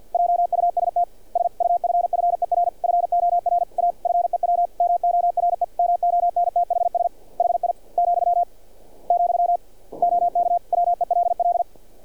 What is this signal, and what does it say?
Morse code, heard as a keyed tone, about 46 wpm, 700 Hz. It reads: JUST SUFFER FOR A FEW MORE MONTHS HI , , OK WELL
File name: archived / CRSnet